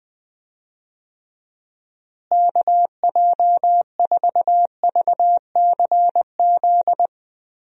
Morse code, heard as a keyed tone, about 20 wpm, 700 Hz. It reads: KJ4VCZ
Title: USA Call Signs - 6 Characters - Repeated 2x - 20wpm